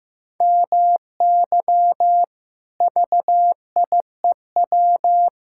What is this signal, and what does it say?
Morse code, heard as a keyed tone, about 14 wpm, 700 Hz. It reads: MY VIEW
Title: Sentences from Top 1000 Words - Review Part 6 - 15wpm